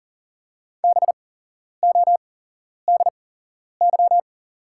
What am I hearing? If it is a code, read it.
Morse code, heard as a keyed tone, about 40 wpm, spaced as for 11 wpm, 690 Hz. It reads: BODY
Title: 8x Character Spacing - Top 500 Words - 40wpm